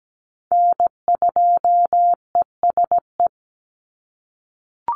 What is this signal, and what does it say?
Morse code, heard as a keyed tone, about 17 wpm, 700 Hz. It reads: N2ESE
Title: NA SOTA Chaser Callsigns 17wpm